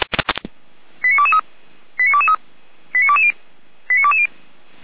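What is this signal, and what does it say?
Five-tone selective call (ZVEI), heard as quick runs of five tones, tones of about 70 ms. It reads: 89223 89223 89220 89220
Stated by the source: ZVEI five-tone selective call 89010 received via SDR, from the feuerwehr-orlamuende fire brigade audio server.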